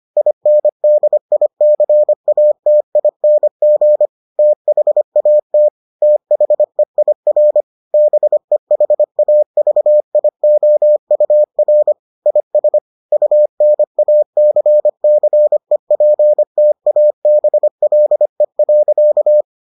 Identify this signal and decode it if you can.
Morse code, heard as a keyed tone, about 25 wpm, 600 Hz. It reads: INDICATING THAT THEIR BEHAVIOUR IS UNACCEPTABLE.